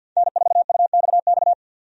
Morse code, heard as a keyed tone, about 46 wpm, 700 Hz. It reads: N4UXX
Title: USA Call Signs - 5 Characters - 50wpm